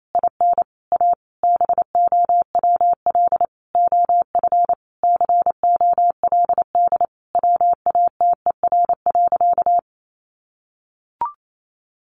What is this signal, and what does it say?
Morse code, heard as a keyed tone, about 28 wpm, 700 Hz. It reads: IN A BOWL OF COLD WATER.
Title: Sentences from Top 1200 Words 28wpm